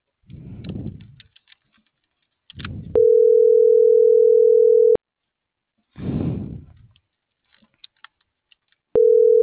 Ringback tone (North America), 440 + 480 Hz.